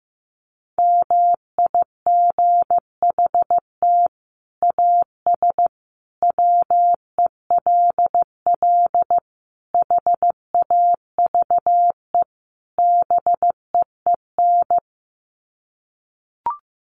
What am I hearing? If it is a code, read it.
Morse code, heard as a keyed tone, about 15 wpm, 700 Hz. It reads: MIGHT AS WELL HAVE BEEN
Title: Sets of 5 Words Encore2 15wpm